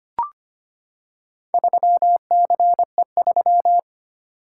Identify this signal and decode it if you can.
Morse code, heard as a keyed tone, about 25 wpm, 700 Hz. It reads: 3CE3
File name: practice / 25wpm / Four Letter-Number Combinations - 25wpm